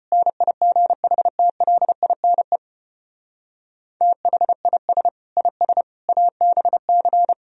Morse code, heard as a keyed tone, about 33 wpm, 700 Hz. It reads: NIGHTLINE THIS IS ABC